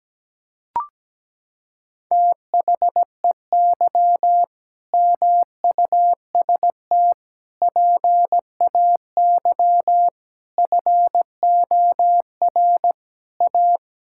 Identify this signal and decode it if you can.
Morse code, heard as a keyed tone, about 17 wpm, 700 Hz. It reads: THEY MUST PAY FOR A
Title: Sentences from Top 1000 Words - Review Part 6 - 17wpm